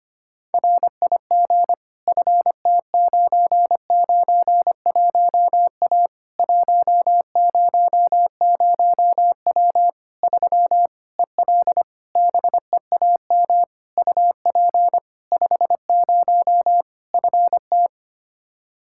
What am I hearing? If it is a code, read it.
Morse code, heard as a keyed tone, about 25 wpm, 700 Hz. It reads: RIG FT991A 100W 3 EL BEAM UP 50 FT
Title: K1USN SST QSO Practice 25wpm